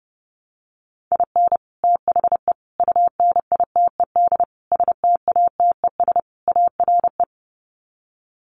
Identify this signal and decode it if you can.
Morse code, heard as a keyed tone, about 30 wpm, 700 Hz. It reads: IN THE UNITED STATES ARE
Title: Sets of 5 Words Encore 30wpm